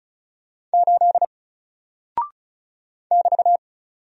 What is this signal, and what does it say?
Morse code, heard as a keyed tone, about 36 wpm, 700 Hz. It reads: 8 <BT>
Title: Single Character (Letter:Number:Punctuation:Prosign) Alt 35wpm